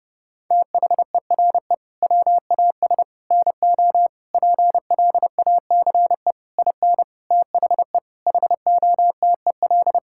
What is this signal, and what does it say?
Morse code, heard as a keyed tone, about 30 wpm, 700 Hz. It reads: THERE WAS NO PLACE IN THE HOTEL
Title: Sentences from Top 1600 Words 30wpm